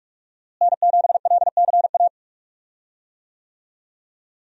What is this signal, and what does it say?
Morse code, heard as a keyed tone, about 46 wpm, 700 Hz. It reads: N7LCA
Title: USA Call Signs - 5 Characters - Repeated 2x - 45wpm